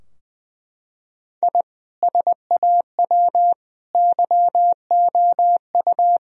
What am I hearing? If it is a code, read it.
Morse code, heard as a keyed tone, about 20 wpm, 700 Hz. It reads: I SAW YOU
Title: Sentences from Top 1000 Words - Review Part 4 - 20wpm